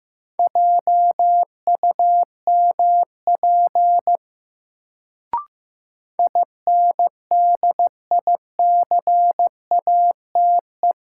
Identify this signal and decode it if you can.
Morse code, heard as a keyed tone, about 15 wpm, 700 Hz. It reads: JUMP INDICATE
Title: Sentences from Top 1300 Words - New Word Drill - 15wpm